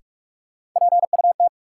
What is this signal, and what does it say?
Morse code, heard as a keyed tone, about 45 wpm, 700 Hz. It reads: PUT
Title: Sentences from Top 200 Words - New Word Drill - 45wpm